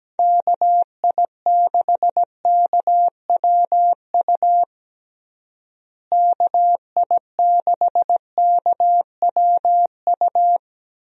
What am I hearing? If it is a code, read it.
Morse code, heard as a keyed tone, about 17 wpm, 700 Hz. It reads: KI6KWU KI6KWU
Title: USA Call Signs - 6 Characters - Repeated 2x - 17wpm